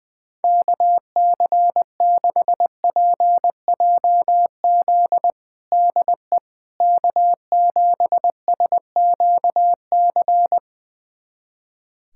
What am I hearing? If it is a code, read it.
Morse code, heard as a keyed tone, about 20 wpm, 700 Hz. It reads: KC6PJZ DE K7SQC
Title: USA Callsign DE Callsign 20wpm